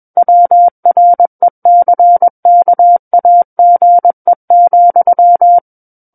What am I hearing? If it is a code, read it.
Morse code, heard as a keyed tone, about 21 wpm, 700 Hz. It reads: WRECKAGE,
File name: Book One / 12-Chapter 12 - 21 WPM